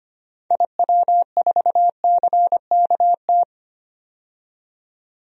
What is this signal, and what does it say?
Morse code, heard as a keyed tone, about 25 wpm, 700 Hz. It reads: IW4CKT
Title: DXCC-100 Call Signs - 6 Characters - 25wpm